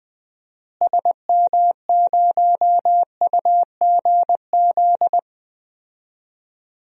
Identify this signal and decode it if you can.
Morse code, heard as a keyed tone, about 20 wpm, 700 Hz. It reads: SM0UGZ
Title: DXCC-100 Call Signs - 6 Characters - Repeated 3x - 20wpm